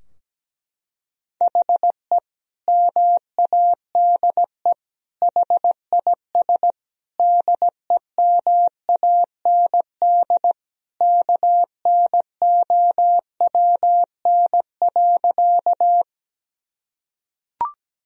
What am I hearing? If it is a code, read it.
Morse code, heard as a keyed tone, about 17 wpm, 700 Hz. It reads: HE MADE HIS DEMAND KNOWN.
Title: Sentences from Top 1000 Words - Review Part 4 - 17wpm